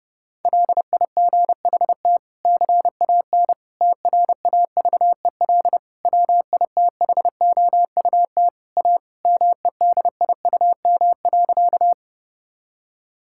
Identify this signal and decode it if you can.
Morse code, heard as a keyed tone, about 32 wpm, 700 Hz. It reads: LIGHT CAN TRAVEL WITHOUT A MEDIUM.
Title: Sentences from Top 1000 Words - Review Part 6 - 30wpm